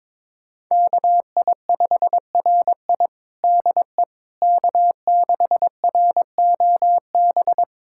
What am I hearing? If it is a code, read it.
Morse code, heard as a keyed tone, about 23 wpm, 700 Hz. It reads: KI5RI DE K6ROB
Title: USA Callsign DE Callsign - 5 Characters - 22wpm